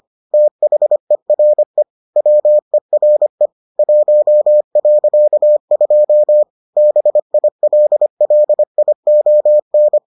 Morse code, heard as a keyed tone, about 25 wpm, 600 Hz. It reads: THERE WERE 1.2 BILLION